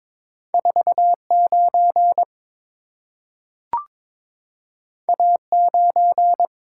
Morse code, heard as a keyed tone, about 22 wpm, 700 Hz. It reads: 49 A9
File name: practice / 22wpm / Two Letter-Number Combinations - 22wpm